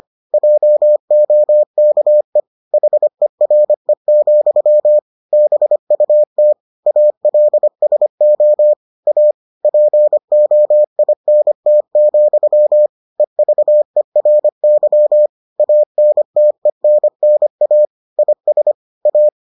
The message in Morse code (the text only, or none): JOKE HERE, BUT ALSO A POINT, EVERY ANTENNA IS A